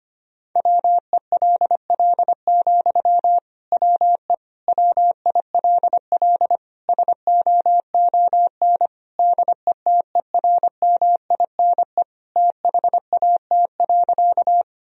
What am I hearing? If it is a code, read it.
Morse code, heard as a keyed tone, about 25 wpm, 700 Hz. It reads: WELL, WE WILL SOON DETERMINE THAT.
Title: Sentences from Top 1000 Words - Review Part 4 - 25wpm